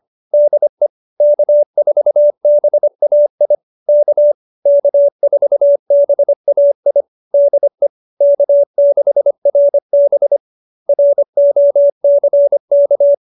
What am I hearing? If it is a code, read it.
Morse code, heard as a keyed tone, about 25 wpm, 570 Hz. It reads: DE K4BAI K K4BAI DE K6RB ROCK